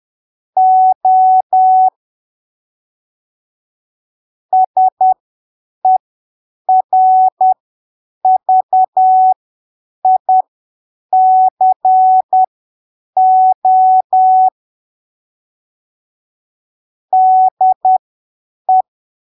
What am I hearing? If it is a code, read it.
Morse code, heard as a keyed tone, about 10 wpm, spaced as for 7 wpm, 750 Hz. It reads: O SERVICO DE
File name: arquivos / texto / texto 01 10ppm